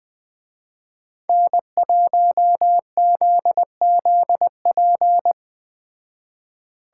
Morse code, heard as a keyed tone, about 20 wpm, 700 Hz. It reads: N1ZZP